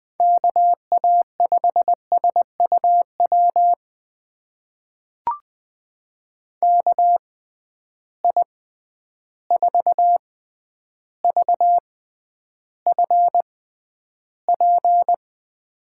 Morse code, heard as a keyed tone, about 20 wpm, spaced as for 10 wpm, 700 Hz. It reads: KA5SUW KI4VFP
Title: USA Call Signs - 6 Characters - 6x Character Spacing - 20wpm